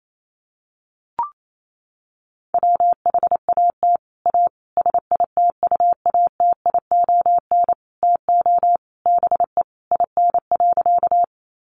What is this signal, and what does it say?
Morse code, heard as a keyed tone, about 28 wpm, 700 Hz. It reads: WHAT A SITUATION TO BE IN.
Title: Sentences from Top 1200 Words 28wpm